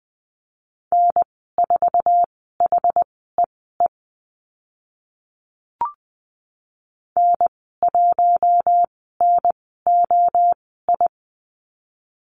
Morse code, heard as a keyed tone, about 20 wpm, spaced as for 14 wpm, 700 Hz. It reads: N4HEE N1NOI
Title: 076 Lesson 13g - Letter L - Call Signs 20wpm